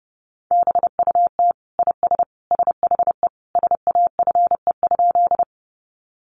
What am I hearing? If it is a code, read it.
Morse code, heard as a keyed tone, about 28 wpm, 700 Hz. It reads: BUT IS SHE SAFE?
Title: Sentences from Top 900 Words 30wpm